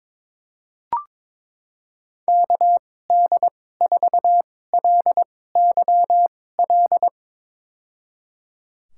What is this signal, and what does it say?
Morse code, heard as a keyed tone, about 22 wpm, spaced as for 16 wpm, 700 Hz. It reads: KD4LYL
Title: USA Call Signs - 6 Characters - 2x Character Spacing - 22wpm